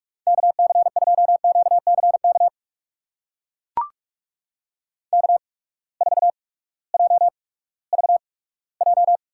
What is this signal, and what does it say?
Morse code, heard as a keyed tone, about 44 wpm, spaced as for 20 wpm, 700 Hz. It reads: KC2QCK K4JVJ
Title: USA Call Signs - 8x Character Spacing - 45wpm